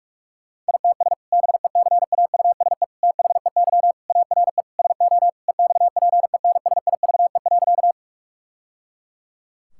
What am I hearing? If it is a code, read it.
Morse code, heard as a keyed tone, about 48 wpm, 700 Hz. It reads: ITS BECAUSE THEY ARE SO EXPENSIVE.
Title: Sentences from Top 1000 Words - Review Part 6 - 45wpm